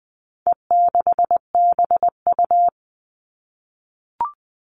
Morse code, heard as a keyed tone, about 20 wpm, 700 Hz. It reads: E6BU
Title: All Cut Numbers - 4-Digit 20wpm